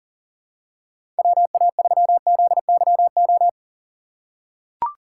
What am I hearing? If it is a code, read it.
Morse code, heard as a keyed tone, about 40 wpm, 700 Hz. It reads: WA3ZYO
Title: USA Call Signs - 6 Characters - 3x Character Spacing - 40wpm